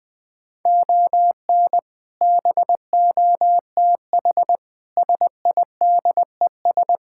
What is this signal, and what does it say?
Morse code, heard as a keyed tone, about 21 wpm, 700 Hz. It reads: ON BOTH SIDES
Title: Sets of 6 Words - 20wpm